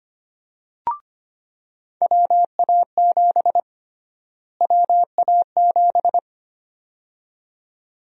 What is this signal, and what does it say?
Morse code, heard as a keyed tone, about 25 wpm, 700 Hz. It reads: WA7 WA7
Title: USA Call Sign Prefixes - Repeated 2x - 25wpm